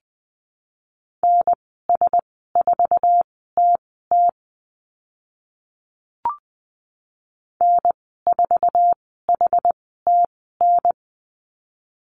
Morse code, heard as a keyed tone, about 20 wpm, spaced as for 14 wpm, 700 Hz. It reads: NS4TT N4HTN
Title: 069 Lesson 12g - Letter d - Call Signs 20wpm